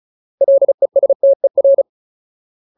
Morse code, heard as a keyed tone, about 34 wpm, 550 Hz. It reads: LESTER
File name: Ultimate Word List / 35WPM Ultimate